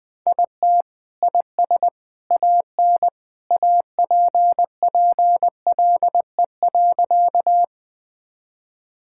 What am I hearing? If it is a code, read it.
Morse code, heard as a keyed tone, about 20 wpm, 700 Hz. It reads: IT IS AN APPLE.
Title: Sentences with 2-Word Phrases - Spoken Beginning - 20wpm